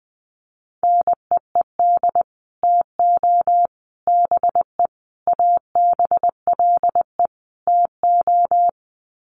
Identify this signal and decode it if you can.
Morse code, heard as a keyed tone, about 20 wpm, 700 Hz. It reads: NEED TO BE ABLE TO
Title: Sets of 5 Words Encore 20wpm